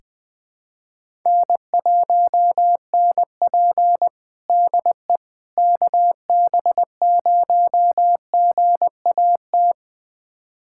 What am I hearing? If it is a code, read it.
Morse code, heard as a keyed tone, about 20 wpm, 700 Hz. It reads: N1NP DE KB0GAT
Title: USA Callsign DE Callsign 20wpm